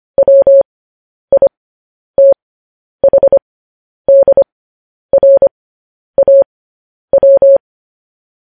Morse code, heard as a keyed tone, about 25 wpm, spaced as for 10 wpm, 550 Hz. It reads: WITHDRAW
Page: News Headlines in Morse Code at 10 WPM